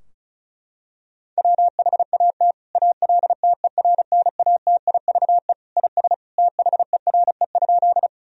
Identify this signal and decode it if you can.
Morse code, heard as a keyed tone, about 36 wpm, 700 Hz. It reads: WHAT ALTERNATIVE IS THERE?
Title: Sentences from Top 1000 Words - Review Part 4 - 35wpm